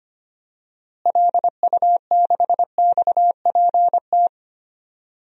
Morse code, heard as a keyed tone, about 25 wpm, 700 Hz. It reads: LU6XPT
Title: DXCC-100 Call Signs - 6 Characters - Repeated 2x - 25wpm